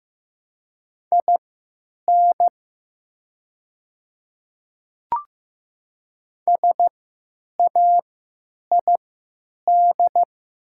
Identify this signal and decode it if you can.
Morse code, heard as a keyed tone, about 15 wpm, spaced as for 9 wpm, 700 Hz. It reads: IN SAID
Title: 3x Character Spacing - Top 100 Words - 15wpm